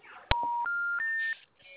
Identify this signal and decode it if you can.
Special information tone, with speech.